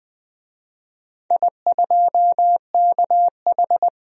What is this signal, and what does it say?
Morse code, heard as a keyed tone, about 21 wpm, 700 Hz. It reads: I2KH
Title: DXCC-100 Call Signs - 4 Characters - 20wpm